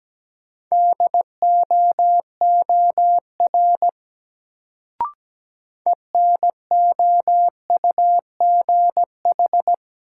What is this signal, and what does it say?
Morse code, heard as a keyed tone, about 17 wpm, 700 Hz. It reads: DOOR ENOUGH
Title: Sets of 2 Words - New Word Drill - 17wpm